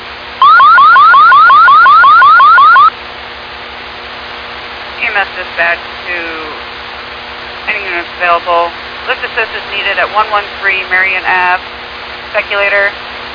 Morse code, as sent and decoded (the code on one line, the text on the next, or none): none